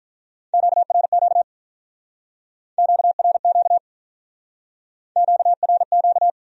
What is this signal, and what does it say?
Morse code, heard as a keyed tone, about 43 wpm, 700 Hz. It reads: QRQ QRQ QRQ
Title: Speed-Racing Q-Codes - 35wpm